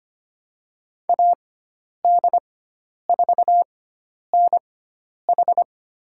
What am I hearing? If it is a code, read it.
Morse code, heard as a keyed tone, about 25 wpm, spaced as for 10 wpm, 700 Hz. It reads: AD4NH
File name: Lesson 16g - Letter c - Call Signs 25wpm